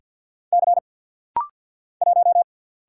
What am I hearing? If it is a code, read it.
Morse code, heard as a keyed tone, about 50 wpm, 700 Hz. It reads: C 1
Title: Single Letter-Number - Rapid-Fire 50wpm